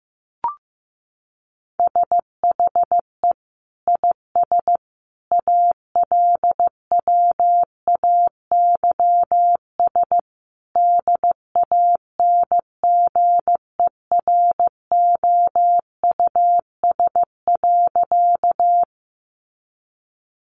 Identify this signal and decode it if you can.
Morse code, heard as a keyed tone, about 15 wpm, 700 Hz. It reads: SHE IS ALWAYS DANGEROUS.
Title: Sentences from Top 1200 Words 15wpm